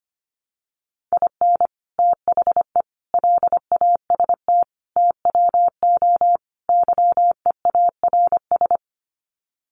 Morse code, heard as a keyed tone, about 25 wpm, 700 Hz. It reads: IN THE LAST TWO YEARS
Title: Sets of 5 Words 25wpm